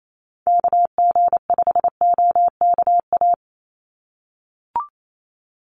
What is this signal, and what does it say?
Morse code, heard as a keyed tone, about 28 wpm, 700 Hz. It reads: KG5OKA
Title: USA Call Signs 28wpm